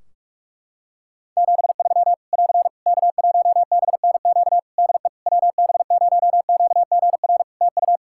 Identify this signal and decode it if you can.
Morse code, heard as a keyed tone, about 46 wpm, 700 Hz. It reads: 73 <AR> K1BNQ DE WB0QGR TU
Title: Practice QSO - Exchange 4 - Ending - 45wpm